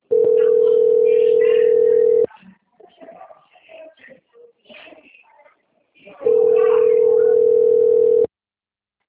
Ringback tone (North America), 440 + 480 Hz.